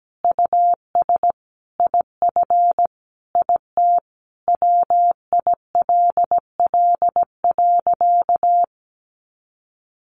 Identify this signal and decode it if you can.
Morse code, heard as a keyed tone, about 17 wpm, 700 Hz. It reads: US IF IT WILL.
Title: Sentences from Top 1500 Words 17wpm